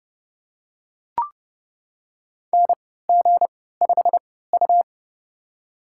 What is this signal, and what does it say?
Morse code, heard as a keyed tone, about 30 wpm, spaced as for 17 wpm, 700 Hz. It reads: NG5U
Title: USA Call Signs - 3x Character Spacing - 30wpm